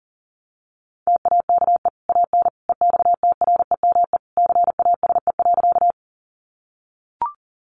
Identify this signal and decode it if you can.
Morse code, heard as a keyed tone, about 40 wpm, 700 Hz. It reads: TAKE AN EXTREME CASE.
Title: Sentences from Top 1300 Words 40wpm